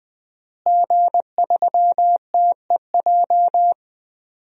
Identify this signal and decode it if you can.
Morse code, heard as a keyed tone, about 20 wpm, 700 Hz. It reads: G3TEJ
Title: DXCC-100 Call Signs - 5 Characters - Repeated 3x - 20wpm